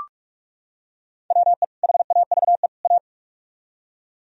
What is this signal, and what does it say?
Morse code, heard as a keyed tone, about 44 wpm, 700 Hz. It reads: WE HAVE A